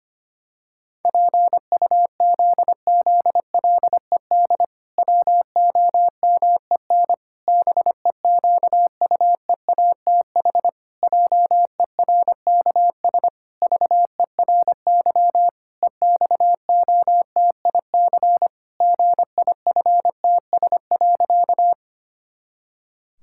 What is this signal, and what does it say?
Morse code, heard as a keyed tone, about 25 wpm, 700 Hz. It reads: PUZZLED WOMEN BEQUEATH JERKS VERY EXOTIC GIFTS.
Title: Pangram Challenge v2 - 25wpm